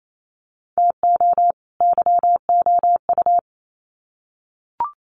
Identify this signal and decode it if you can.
Morse code, heard as a keyed tone, about 28 wpm, 700 Hz. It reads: TO YOU
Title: Sets of 2 Words 28wpm